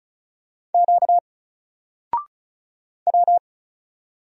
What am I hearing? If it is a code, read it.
Morse code, heard as a keyed tone, about 33 wpm, 700 Hz. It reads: Q W
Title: Single Letter-Number 35wpm